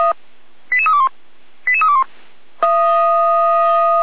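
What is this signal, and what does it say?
Five-tone selective call (ZVEI), heard as quick runs of five tones, tones of about 70 ms. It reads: 88321 88321
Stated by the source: ZVEI five-tone call 88127 received by SDR.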